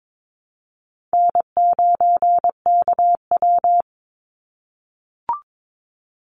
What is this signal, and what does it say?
Morse code, heard as a keyed tone, about 22 wpm, 700 Hz. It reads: N9KW
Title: NA SOTA Chaser Callsigns 22wpm